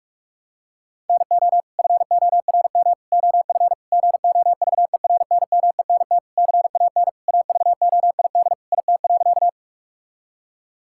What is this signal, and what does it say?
Morse code, heard as a keyed tone, about 46 wpm, 700 Hz. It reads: NO FORM OF GOVERNMENT CAN AVOID IT.
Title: Sentences from Top 1000 Words - Review Part 6 - 45wpm